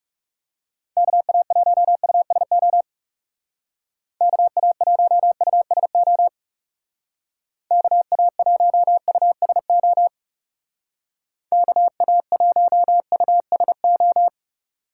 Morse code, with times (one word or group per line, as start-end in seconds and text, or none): 1.0-2.8 KA1USO
4.2-6.3 KA1USO
7.7-10.1 KA1USO
11.5-14.3 KA1USO